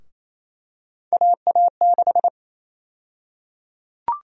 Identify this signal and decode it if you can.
Morse code, heard as a keyed tone, about 28 wpm, 700 Hz. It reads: AA6
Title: USA Call Sign Prefixes - Repeated 2x - 28wpm